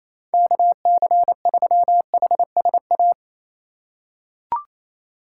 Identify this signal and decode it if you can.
Morse code, heard as a keyed tone, about 28 wpm, 700 Hz. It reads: KC3HSA
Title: USA Call Signs - 6 Characters - 2x Character Spacing - 28wpm